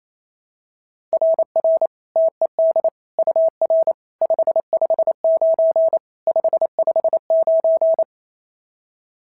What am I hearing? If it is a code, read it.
Morse code, heard as a keyed tone, about 28 wpm, 690 Hz. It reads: RR TED UR 559 559
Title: SOTA QSO Practice 28wpm